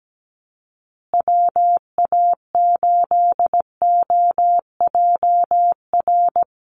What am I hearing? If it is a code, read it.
Morse code, heard as a keyed tone, about 17 wpm, 700 Hz. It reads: WA8OJR
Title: Call Signs Repeated 2x - 17wpm